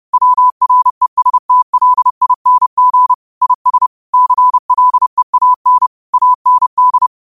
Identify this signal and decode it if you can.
Morse code, heard as a keyed tone, about 30 wpm, 1000 Hz. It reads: WRESTLING IS CLEAN AND